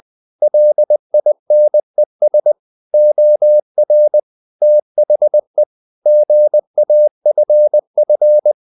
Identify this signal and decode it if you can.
Morse code, heard as a keyed tone, about 20 wpm, 600 Hz. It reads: LINES OR THE GAFF
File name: the old man and the sea / the old man and the sea part 1 20wpm